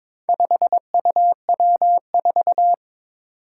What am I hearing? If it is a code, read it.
Morse code, heard as a keyed tone, about 22 wpm, 700 Hz. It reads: 5UW4
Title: Four Challenging Letter-Number Combinations 22wpm